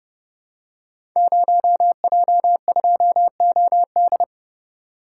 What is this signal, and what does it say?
Morse code, heard as a keyed tone, about 32 wpm, 700 Hz. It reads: 0J2OD
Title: Code Groups - 5 Letter:Numbers - 30wpm